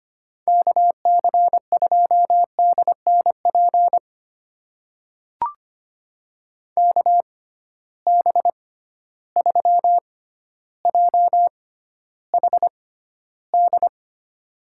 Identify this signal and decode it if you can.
Morse code, heard as a keyed tone, about 25 wpm, spaced as for 13 wpm, 700 Hz. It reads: KC2DNP KB3JHD
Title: USA Call Signs - 6 Characters - 6x Character Spacing - 25wpm